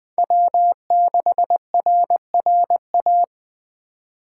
Morse code, heard as a keyed tone, about 21 wpm, 700 Hz. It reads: W6RRA